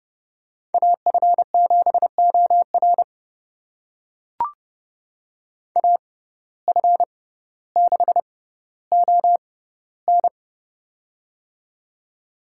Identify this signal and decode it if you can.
Morse code, heard as a keyed tone, about 32 wpm, spaced as for 16 wpm, 690 Hz. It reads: AF7OR AF6ON
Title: USA Call Signs - 5 Characters - 6x Character Spacing - 30wpm